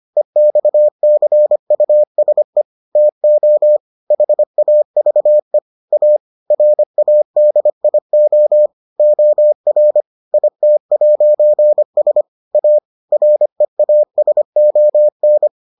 Morse code, heard as a keyed tone, about 25 wpm, 600 Hz. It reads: EXCUSE TO HAVE A RADIO OR IT'S A REASON